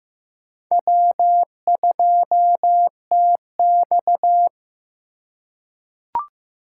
Morse code, heard as a keyed tone, about 15 wpm, 700 Hz. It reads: W2TX
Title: USA Call Signs - 4 Characters - 2x Character Spacing - 15wpm